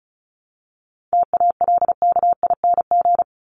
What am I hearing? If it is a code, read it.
Morse code, heard as a keyed tone, about 34 wpm, 700 Hz. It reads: TALKING
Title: ING Suffix - Top 100 Words with Base Words 35wpm